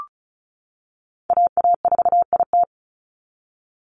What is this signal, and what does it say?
Morse code, heard as a keyed tone, about 36 wpm, 700 Hz. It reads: AA4IT
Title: NA SOTA Chaser Callsigns 35wpm